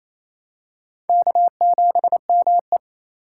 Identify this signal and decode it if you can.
Morse code, heard as a keyed tone, about 28 wpm, 700 Hz. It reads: K7ME
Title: USA Call Signs - 4 Characters - 3x Character Spacing - 28wpm